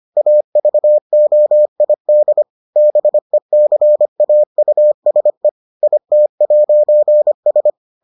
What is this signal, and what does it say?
Morse code, heard as a keyed tone, about 25 wpm, 600 Hz. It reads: AVOID BECAUSE IT'S